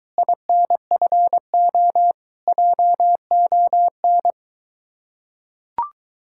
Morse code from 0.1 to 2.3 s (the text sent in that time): INFO